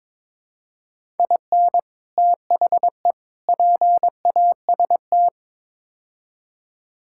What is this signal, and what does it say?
Morse code, heard as a keyed tone, about 22 wpm, 700 Hz. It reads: IN THE PAST
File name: practice / 22wpm / Build A Phrase - 5-Word Phrases - 22wpm